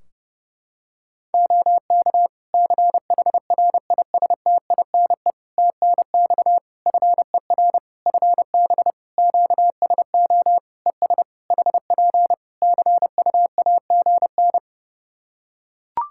Morse code, heard as a keyed tone, about 30 wpm, 700 Hz. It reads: OK CHRISTINE TNX FER FB QSO ES HP CUAGN
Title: Practice QSO - Exchange 4 - Ending - 30wpm